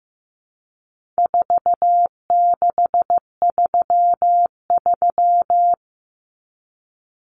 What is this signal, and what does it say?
Morse code, heard as a keyed tone, about 14 wpm, 700 Hz. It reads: 4633
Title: Common Cut Numbers - 4-Digit 15wpm